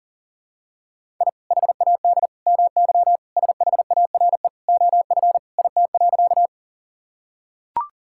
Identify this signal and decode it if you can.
Morse code, heard as a keyed tone, about 40 wpm, 700 Hz. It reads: I HAD MY SHARE OF IT.
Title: Sentences from Top 1000 Words - Review Part 6 - 40wpm